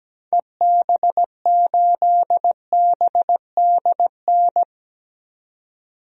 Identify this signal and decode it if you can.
Morse code, heard as a keyed tone, about 17 wpm, 700 Hz. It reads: EB8BDN
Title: DXCC-100 Call Signs - 6 Characters - 17wpm